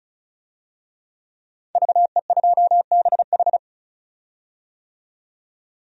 Morse code, heard as a keyed tone, about 36 wpm, 700 Hz. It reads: VE2BH
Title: DXCC-100 Call Signs - 5 Characters - 35wpm